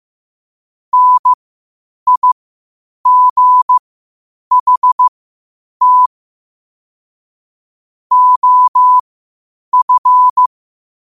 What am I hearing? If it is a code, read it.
Morse code, heard as a keyed tone, about 15 wpm, spaced as for 8 wpm, 1000 Hz. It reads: NIGHT OF
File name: qotdfiles / 8wpm